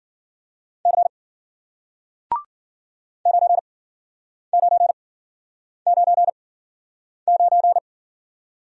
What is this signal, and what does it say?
Morse code, heard as a keyed tone, about 53 wpm, spaced as for 9 wpm, 700 Hz. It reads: C 9999